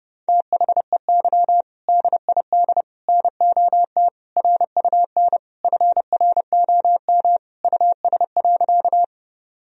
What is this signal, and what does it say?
Morse code, heard as a keyed tone, about 30 wpm, 700 Hz. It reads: THEY DID NOT RUN FROM US.
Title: Sentences from Top 1000 Words - Review Part 1 - 30wpm